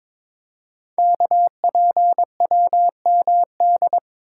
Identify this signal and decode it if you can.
Morse code, heard as a keyed tone, about 22 wpm, 700 Hz. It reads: KPWMD
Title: Code Groups - 5 Letters - 22wpm